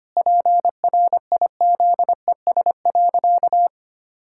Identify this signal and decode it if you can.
Morse code, heard as a keyed tone, about 25 wpm, 700 Hz. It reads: PRIZES.
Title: Pangram Challenge v2 - 25wpm